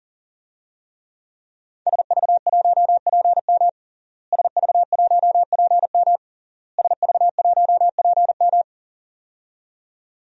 Morse code, heard as a keyed tone, about 40 wpm, 700 Hz. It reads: SV1PM SV1PM SV1PM
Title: DXCC-100 Call Signs - 5 Characters - Repeated 3x - 40wpm